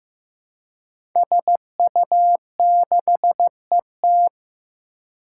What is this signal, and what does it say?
Morse code, heard as a keyed tone, about 15 wpm, 700 Hz. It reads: SU6ET
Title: Five Challenging Letter-Number Combinations 15wpm